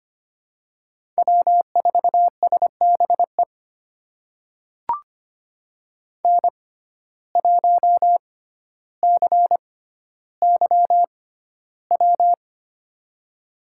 Morse code, heard as a keyed tone, about 25 wpm, spaced as for 13 wpm, 700 Hz. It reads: W4SBE N1CYW
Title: USA Call Signs - 5 Characters - 6x Character Spacing - 25wpm